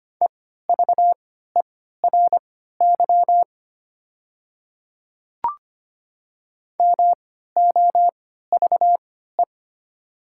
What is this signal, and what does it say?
Morse code, heard as a keyed tone, about 25 wpm, spaced as for 14 wpm, 700 Hz. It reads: EVERY MOVE
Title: 3x Character Spacing - Top 500 Words - 25wpm